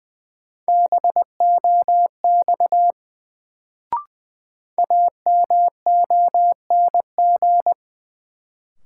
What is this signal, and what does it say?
Morse code, heard as a keyed tone, about 21 wpm, 700 Hz. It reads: BOX AMONG
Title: Sentences from Top 500 Words - New Word Drill - 20wpm